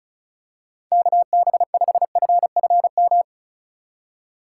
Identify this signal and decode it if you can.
Morse code, heard as a keyed tone, about 36 wpm, 700 Hz. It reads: KB5FFM
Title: USA Call Signs - 4x Character Spacing - 35wpm